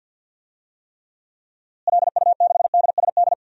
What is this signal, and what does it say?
Morse code, heard as a keyed tone, about 52 wpm, 700 Hz. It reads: LU6DSD
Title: DXCC-100 Call Signs - 6 Characters - Repeated 3x - 50wpm